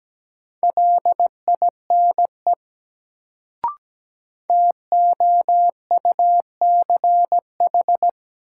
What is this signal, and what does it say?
Morse code, heard as a keyed tone, about 17 wpm, 700 Hz. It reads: LINE TOUCH
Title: Sentences from Top 1200 Words - New Word Drill Alt - 17wpm